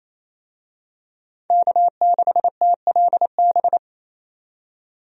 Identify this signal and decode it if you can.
Morse code, heard as a keyed tone, about 28 wpm, 700 Hz. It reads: K6TLB
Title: USA Call Signs - 5 Characters - 28wpm